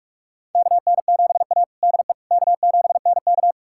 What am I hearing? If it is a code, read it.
Morse code, heard as a keyed tone, about 46 wpm, 700 Hz. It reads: KN7A DE K7NK
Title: USA Callsign DE Callsign - 4 Characters - 45wpm